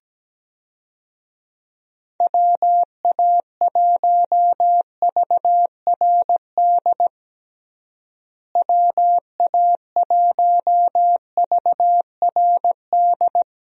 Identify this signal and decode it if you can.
Morse code, heard as a keyed tone, about 17 wpm, 700 Hz. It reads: WA1VRD WA1VRD
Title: USA Call Signs - 6 Characters - Repeated 3x - 17wpm